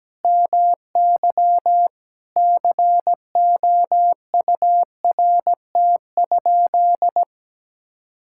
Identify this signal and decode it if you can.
Morse code, heard as a keyed tone, about 17 wpm, 700 Hz. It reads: MY COURT?
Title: Sentences from Top 1600 Words 17wpm